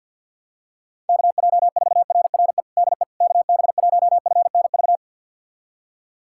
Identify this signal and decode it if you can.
Morse code, heard as a keyed tone, about 52 wpm, 700 Hz. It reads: KJ4RRE DE KB1FNV
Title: USA Callsign DE Callsign - 6 Characters - 50wpm